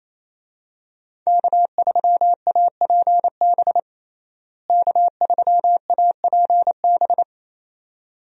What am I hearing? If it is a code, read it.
Morse code, heard as a keyed tone, about 28 wpm, 700 Hz. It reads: K3APB K3APB
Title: USA Call Signs - 5 Characters - Repeated 2x - 28wpm